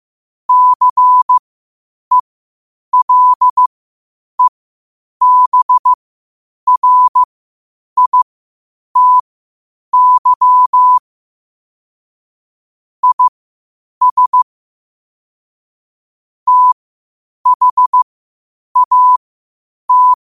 Morse code, heard as a keyed tone, about 15 wpm, spaced as for 8 wpm, 1000 Hz. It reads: CELEBRITY IS THAT
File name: qotdfiles / 8wpm